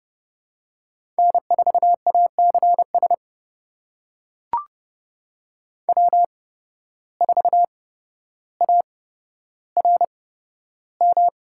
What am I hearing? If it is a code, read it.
Morse code, heard as a keyed tone, about 32 wpm, spaced as for 13 wpm, 700 Hz. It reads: N4ACS W4ARM
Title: USA Call Signs - 5 Characters - 8x Character Spacing - 30wpm